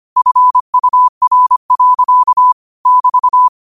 Morse code, heard as a keyed tone, about 25 wpm, 1000 Hz. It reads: FUR. <BT>
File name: 25wpm